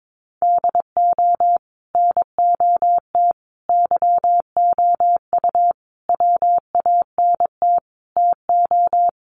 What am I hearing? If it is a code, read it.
Morse code, heard as a keyed tone, about 22 wpm, 700 Hz. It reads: DO NOT YOU WANT TO
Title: Sets of 5 Words Encore2 22wpm